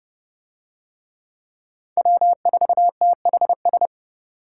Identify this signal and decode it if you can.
Morse code, heard as a keyed tone, about 28 wpm, 700 Hz. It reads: W4THS